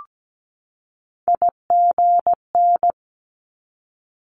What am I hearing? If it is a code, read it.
Morse code, heard as a keyed tone, about 17 wpm, 700 Hz. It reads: IGN
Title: Multi-letter Phonemes 17wpm